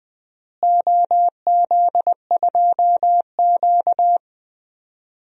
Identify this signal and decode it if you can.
Morse code, heard as a keyed tone, about 21 wpm, 700 Hz. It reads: OZ2Q
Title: DXCC-100 Call Signs - 4 Characters - Repeated 3x - 20wpm